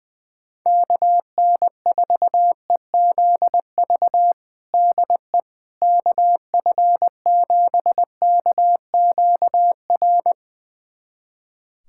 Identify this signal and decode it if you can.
Morse code, heard as a keyed tone, about 20 wpm, 700 Hz. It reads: KN4EZV DE KF7KQR
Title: USA Callsign DE Callsign 20wpm